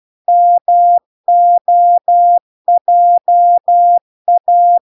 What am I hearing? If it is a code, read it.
Morse code, heard as a keyed tone, about 12 wpm, 700 Hz. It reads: MOJA